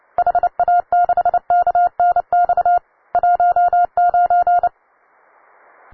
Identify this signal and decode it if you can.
Morse code, heard as a keyed tone, about 30 wpm, 710 Hz. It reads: HA6KNX 19
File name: wae / cw beginner / ua9cdv cw